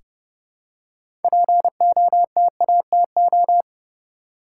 Morse code, heard as a keyed tone, about 30 wpm, 700 Hz. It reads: POTATO
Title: Six Letter Words - 30wpm